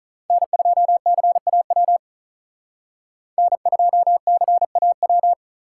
Morse code, heard as a keyed tone, about 46 wpm, 700 Hz. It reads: N2CAW N2CAW